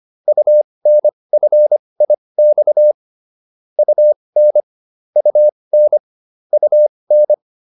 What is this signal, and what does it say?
Morse code, heard as a keyed tone, about 25 wpm, spaced as for 20 wpm, 600 Hz. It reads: UNFIX UN UN UN